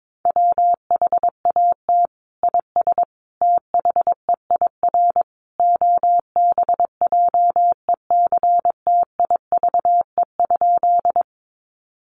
Morse code, heard as a keyed tone, about 22 wpm, 700 Hz. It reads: WHAT IS THEIR OBJECTIVE?
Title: Sentences from Top 1200 Words 22wpm